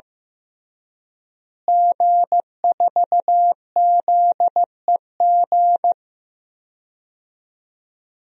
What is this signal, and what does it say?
Morse code, heard as a keyed tone, about 16 wpm, 700 Hz. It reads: G4ZEG
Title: DXCC-100 Call Signs - 5 Characters - Repeated 3x - 15wpm